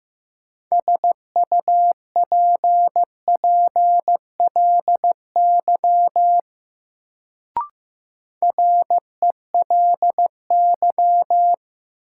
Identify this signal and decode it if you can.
Morse code, heard as a keyed tone, about 15 wpm, 700 Hz. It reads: SUPPLY RELY
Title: Sentences from Top 1200 Words - New Word Drill Alt - 15wpm